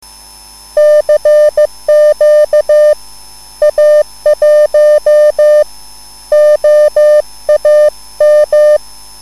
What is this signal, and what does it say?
Morse code, heard as a keyed tone, about 15 wpm, spaced as for 13 wpm, 600 Hz. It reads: CQ A1OAM
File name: cwbook